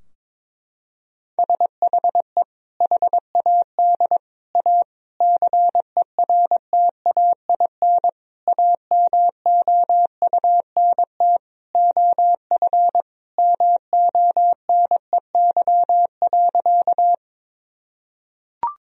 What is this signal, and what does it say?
Morse code, heard as a keyed tone, about 22 wpm, 700 Hz. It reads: SHE HAD A CERTAIN AMOUNT OF MONEY.
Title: Sentences from Top 1000 Words - Review Part 4 - 22wpm